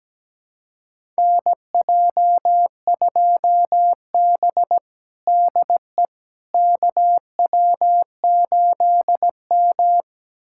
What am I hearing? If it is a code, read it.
Morse code, heard as a keyed tone, about 17 wpm, 700 Hz. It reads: NJ2B DE KW8M
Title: USA Callsign DE Callsign - 4 Characters - 17wpm